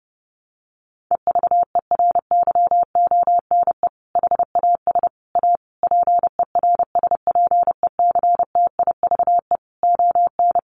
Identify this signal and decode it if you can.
Morse code, heard as a keyed tone, about 30 wpm, 700 Hz. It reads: EVERYONE HAS A PERSPECTIVE ON
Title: Sentences from Top 1200 Words 30wpm